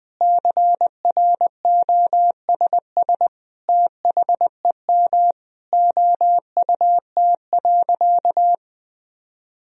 Morse code, heard as a keyed tone, about 20 wpm, 700 Hz. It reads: CROSS THEM OUT.
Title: Sentences from Top 1000 Words - Review Part 1 - 20wpm